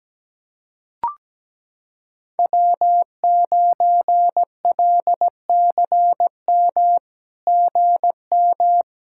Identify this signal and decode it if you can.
Morse code, heard as a keyed tone, about 17 wpm, 700 Hz. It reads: W9LCM GM
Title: SOTA QSO Practice 17wpm